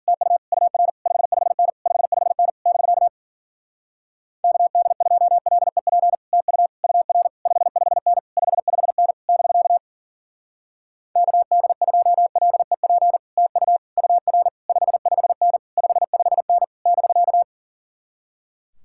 Morse code, heard as a keyed tone, about 52 wpm, 700 Hz. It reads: TU UR 55N 55N <BK> KD2LEP TU UR 55N 55N <BK> KD2LEP TU UR 55N 55N <BK>